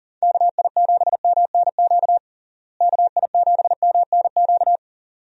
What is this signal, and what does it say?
Morse code, heard as a keyed tone, about 40 wpm, 700 Hz. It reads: KI7MNQ KI7MNQ
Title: USA Call Signs - 6 Characters - Repeated 2x - 40wpm